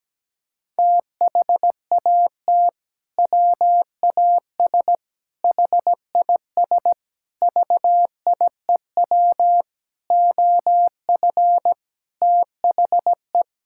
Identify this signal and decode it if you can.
Morse code, heard as a keyed tone, about 17 wpm, 700 Hz. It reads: THAT WAS HIS VIEW OF THE
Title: Sentences from Top 1000 Words - Review Part 6 - 17wpm